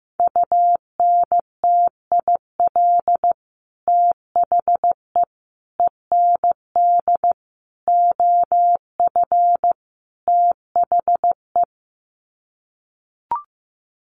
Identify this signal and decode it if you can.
Morse code, heard as a keyed tone, about 15 wpm, 700 Hz. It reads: UNTIL THE END OF THE
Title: Sets of 5 Words Encore 15wpm